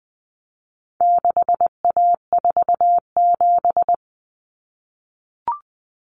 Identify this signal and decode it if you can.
Morse code, heard as a keyed tone, about 20 wpm, 700 Hz. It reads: 6A47
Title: Common Cut Numbers - 4-Digit 20wpm